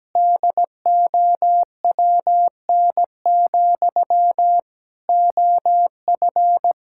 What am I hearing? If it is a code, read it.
Morse code, heard as a keyed tone, about 17 wpm, 700 Hz. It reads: DOWN, OF
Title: Sentences from Top 1000 Words - Review Part 5 - 17wpm